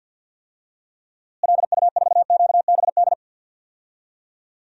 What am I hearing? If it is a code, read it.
Morse code, heard as a keyed tone, about 51 wpm, 700 Hz. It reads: LU4QBD
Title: DXCC-100 Call Signs - 6 Characters - Repeated 3x - 50wpm